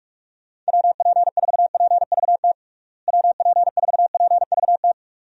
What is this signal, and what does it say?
Morse code, heard as a keyed tone, about 41 wpm, 700 Hz. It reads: WP4PVT WP4PVT
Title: USA Call Signs - 6 Characters - Repeated 2x - 45wpm